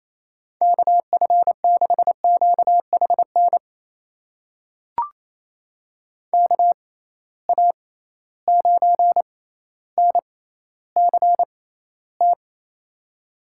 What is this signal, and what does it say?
Morse code, heard as a keyed tone, about 28 wpm, spaced as for 14 wpm, 700 Hz. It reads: KF6QHN KA9NCT
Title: USA Call Signs - 6 Characters - 6x Character Spacing - 28wpm